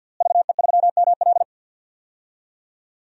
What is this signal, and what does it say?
Morse code, heard as a keyed tone, about 52 wpm, 700 Hz. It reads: VE3ML
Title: DXCC-100 Call Signs - 5 Characters - Repeated 3x - 50wpm